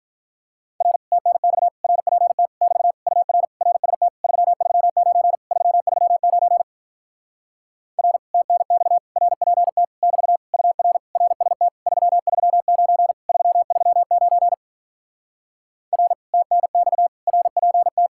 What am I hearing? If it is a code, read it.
Morse code, heard as a keyed tone, about 46 wpm, 700 Hz. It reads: R TNX RPT <BT> UR RST 339 339 R TNX RPT <BT> UR RST 339 339 R TNX RPT